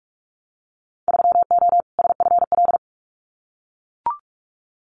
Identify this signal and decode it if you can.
Morse code, heard as a keyed tone, about 44 wpm, 700 Hz. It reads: 3O SFL
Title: Winter Field Day Exchanges 45wpm